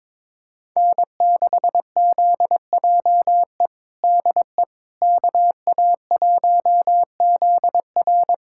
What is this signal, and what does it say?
Morse code, heard as a keyed tone, about 22 wpm, 700 Hz. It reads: N6ZJE DE KA1ZR
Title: USA Callsign DE Callsign - 5 Characters - 22wpm